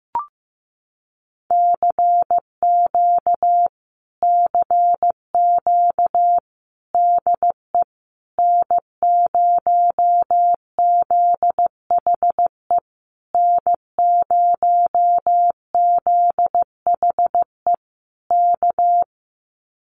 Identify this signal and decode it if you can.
Morse code, heard as a keyed tone, about 15 wpm, 700 Hz. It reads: CQ CQ DE N0ZHE N0ZHE K
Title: Common QSO Phrases Encore 15wpm